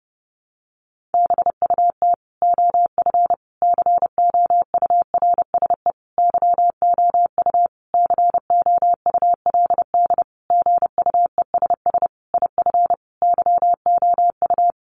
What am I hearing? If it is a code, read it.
Morse code, heard as a keyed tone, about 30 wpm, 700 Hz. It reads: BUT OF COURSE YOU COULD GUESS IF YOU
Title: Sentences from Top 1500 Words 30wpm